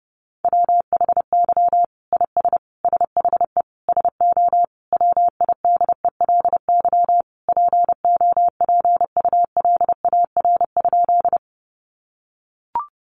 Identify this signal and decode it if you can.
Morse code, heard as a keyed tone, about 28 wpm, 700 Hz. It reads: WHY IS SHE SO WIDELY POPULAR?
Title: Sentences from Top 1200 Words 30wpm